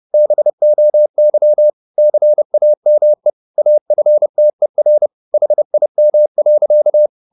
Morse code, heard as a keyed tone, about 30 wpm, 600 Hz. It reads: BOY CAME AFTER HIM.
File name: the old man and the sea / the old man and the sea 1 30wpm